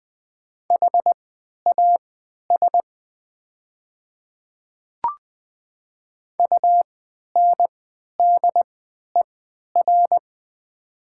Morse code, heard as a keyed tone, about 20 wpm, spaced as for 11 wpm, 700 Hz. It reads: HAS UNDER
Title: 3x Character Spacing - Top 200 Words - 20wpm